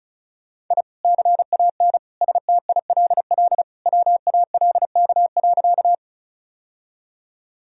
Morse code, heard as a keyed tone, about 34 wpm, 700 Hz. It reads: I CAN STILL WALK.
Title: Sentences from Top 1000 Words - Review Part 1 - 35wpm